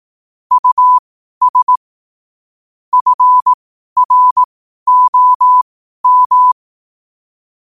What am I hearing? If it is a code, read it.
Morse code, heard as a keyed tone, about 18 wpm, spaced as for 12 wpm, 1000 Hz. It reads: US FROM